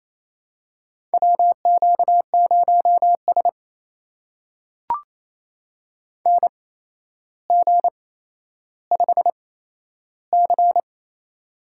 Morse code, heard as a keyed tone, about 28 wpm, spaced as for 12 wpm, 700 Hz. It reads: WQ0S NG5C
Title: USA Call Signs - 4 Characters - 8x Character Spacing - 28wpm